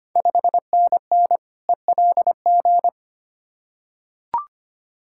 Morse code, heard as a keyed tone, about 25 wpm, 700 Hz. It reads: 5NN ELG
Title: Ontario QSO Party Exchange 25wpm